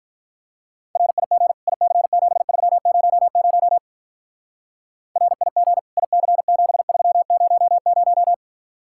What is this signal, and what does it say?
Morse code, heard as a keyed tone, about 43 wpm, 700 Hz. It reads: RIG IC7300 RIG IC7300
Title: Speed-Racing Common QSO Phrases - 35wpm